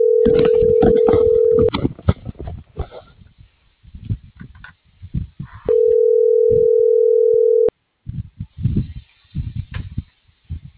Ringback tone (North America), 440 + 480 Hz.